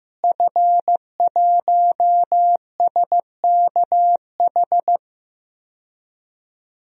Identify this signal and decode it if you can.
Morse code, heard as a keyed tone, about 15 wpm, 700 Hz. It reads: F1SKH
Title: DXCC-100 Call Signs - 5 Characters - Repeated 3x - 15wpm